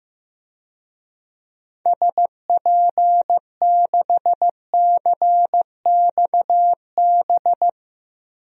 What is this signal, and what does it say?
Morse code, heard as a keyed tone, about 15 wpm, 700 Hz. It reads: SP6CXB
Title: DXCC-100 Call Signs - 6 Characters - 15wpm